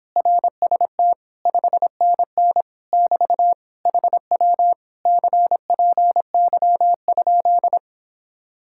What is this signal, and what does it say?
Morse code, heard as a keyed tone, about 31 wpm, 700 Hz. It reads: RST 5NN <BT> HW CPY?